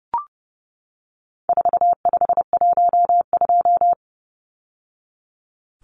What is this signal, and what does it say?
Morse code, heard as a keyed tone, about 30 wpm, 700 Hz. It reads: 4512
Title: Numbers - 4-Digit 30wpm